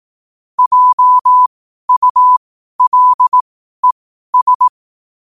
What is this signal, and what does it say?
Morse code, heard as a keyed tone, about 18 wpm, spaced as for 12 wpm, 1000 Hz. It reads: JULES
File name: qotdfiles / 12wpm